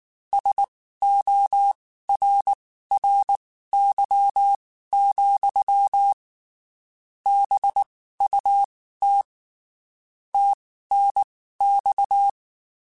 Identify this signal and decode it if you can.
Morse code, heard as a keyed tone, about 19 wpm, spaced as for 14 wpm, 800 Hz. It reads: SORRY, BUT TNX
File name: (spaced) / QSOs Spaced-mp3s / WA7QOX 2 @18 sp